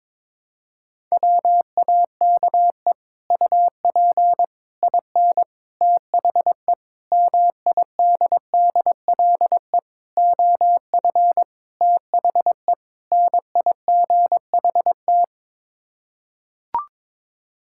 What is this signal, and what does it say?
Morse code, heard as a keyed tone, about 22 wpm, 700 Hz. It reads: WAKE UP IN THE MIDDLE OF THE NIGHT
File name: practice / 22wpm / Sets of 8 Words - 22wpm